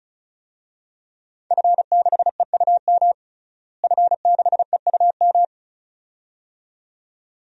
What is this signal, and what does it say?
Morse code, heard as a keyed tone, about 36 wpm, 700 Hz. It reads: F6EUM F6EUM
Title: DXCC-100 Call Signs - 5 Characters - Repeated 2x - 35wpm